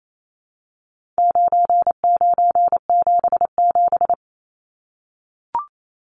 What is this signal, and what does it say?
Morse code, heard as a keyed tone, about 28 wpm, 700 Hz. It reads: 9977
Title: Numbers - 4-Digit 28wpm